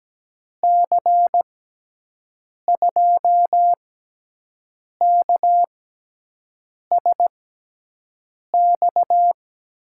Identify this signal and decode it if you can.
Morse code, heard as a keyed tone, about 17 wpm, spaced as for 6 wpm, 700 Hz. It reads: C2KSX
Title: USA Call Signs - 6 Characters - 6x Character Spacing - 17wpm